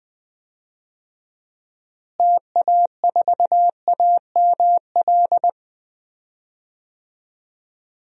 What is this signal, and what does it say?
Morse code, heard as a keyed tone, about 20 wpm, 700 Hz. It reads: TA4AML
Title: DXCC-100 Call Signs - 6 Characters - Repeated 3x - 20wpm